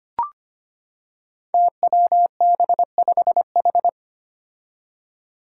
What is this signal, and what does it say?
Morse code, heard as a keyed tone, about 25 wpm, 700 Hz. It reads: TWB5H
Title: Five Challenging Letter-Number Combinations 25wpm